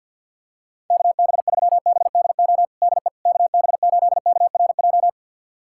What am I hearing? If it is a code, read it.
Morse code, heard as a keyed tone, about 51 wpm, 700 Hz. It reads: KB3BDO DE KB8KRJ
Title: USA Callsign DE Callsign 50wpm